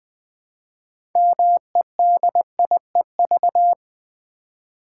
Morse code, heard as a keyed tone, about 20 wpm, 700 Hz. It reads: MEDIEV
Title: Build A Word - 3 Letters At A Time - 8-Letter Words - 20wpm